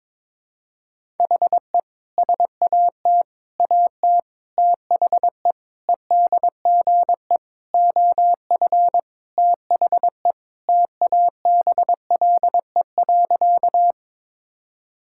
Morse code, heard as a keyed tone, about 22 wpm, 700 Hz. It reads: HE SAT AT THE EDGE OF THE TABLE.
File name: practice / 22wpm / Sentences with 3-Word Phrases - 22wpm